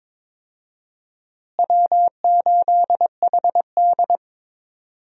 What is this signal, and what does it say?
Morse code, heard as a keyed tone, about 22 wpm, 700 Hz. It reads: W8HD
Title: USA Call Signs - 4 Characters - Repeated 3x - 22wpm